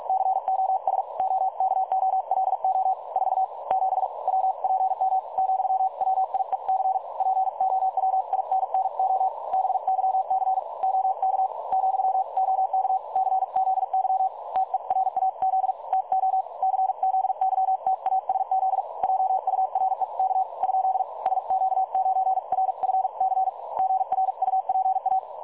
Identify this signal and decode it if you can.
Morse code, heard as a keyed tone, about 47 wpm, spaced as for 41 wpm, 760 Hz. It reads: COS OKOLO 47 WPM , WIEC POWINNO BYC OK , OK CZY TERAZ TO CZYTASZ JUREK ? SQ9RFC GRUPA
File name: dlazaawansowanych / nagraniazpasma / websdr recording 3565.0kHz